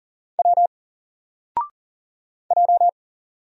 Morse code, heard as a keyed tone, about 40 wpm, 700 Hz. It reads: W J
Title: Single Letter-Number Alt 40wpm